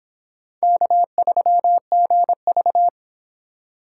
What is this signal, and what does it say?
Morse code, heard as a keyed tone, about 25 wpm, 700 Hz. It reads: K3GV